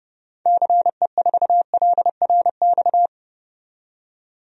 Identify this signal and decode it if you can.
Morse code, heard as a keyed tone, about 31 wpm, 700 Hz. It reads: CE4LRX
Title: DXCC-100 Call Signs - 6 Characters - 30wpm